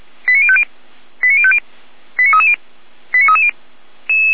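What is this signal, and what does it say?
Five-tone selective call (ZVEI), heard as quick runs of five tones, tones of about 70 ms. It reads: 89050 89050 89330 89330
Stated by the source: ZVEI five-tone selective call received by SDR, from the Feuerwehr Orlamünde fire brigade.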